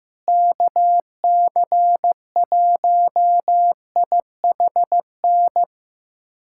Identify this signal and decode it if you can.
Morse code, heard as a keyed tone, about 15 wpm, 700 Hz. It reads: KC1IHN
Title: USA Callsign DE Callsign 15wpm